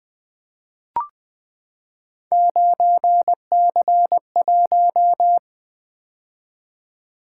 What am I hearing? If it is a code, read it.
Morse code, heard as a keyed tone, about 20 wpm, 700 Hz. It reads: 9C1
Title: Three Letter-Number Combinations - 20wpm